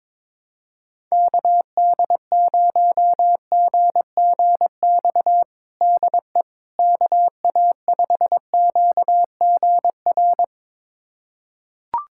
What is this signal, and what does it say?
Morse code, heard as a keyed tone, about 22 wpm, 700 Hz. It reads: KD0GGX DE KA5QGR
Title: USA Callsign DE Callsign 22wpm